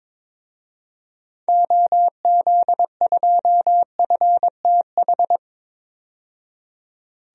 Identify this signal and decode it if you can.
Morse code, heard as a keyed tone, about 22 wpm, 700 Hz. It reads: OZ2FTH
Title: DXCC-100 Call Signs - 6 Characters - 22wpm